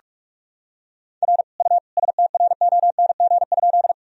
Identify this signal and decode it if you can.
Morse code, heard as a keyed tone, about 46 wpm, 690 Hz. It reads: R U STRONG?
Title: Sentences from Top 500 Words and Abbreviations 45wpm